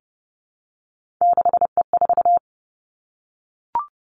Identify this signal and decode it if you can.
Morse code, heard as a keyed tone, about 30 wpm, 700 Hz. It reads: 6E4
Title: Three Challenging Letter-Number Combinations 30wpm